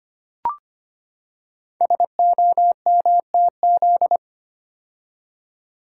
Morse code, heard as a keyed tone, about 25 wpm, 700 Hz. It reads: SOMTZ
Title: Code Groups - 5 Letter:Numbers - 25wpm